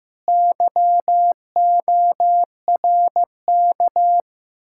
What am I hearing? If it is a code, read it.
Morse code, heard as a keyed tone, about 15 wpm, 700 Hz. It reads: YORK
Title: Speed-Racing Common QSO Phrases Encore - 15wpm